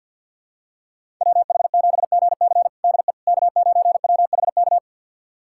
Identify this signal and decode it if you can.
Morse code, heard as a keyed tone, about 52 wpm, 700 Hz. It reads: WH7GC DE K9WHK
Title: USA Callsign DE Callsign - 5 Characters - 50wpm